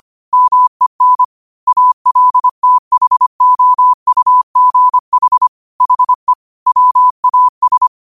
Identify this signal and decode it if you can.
Morse code, heard as a keyed tone, about 25 wpm, 1000 Hz. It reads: MEN ALTHOUGH HE WAS